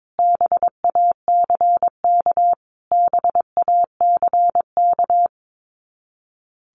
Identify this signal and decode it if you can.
Morse code, heard as a keyed tone, about 22 wpm, 700 Hz. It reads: BACK BACK
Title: Top 200 Words Repeated 22wpm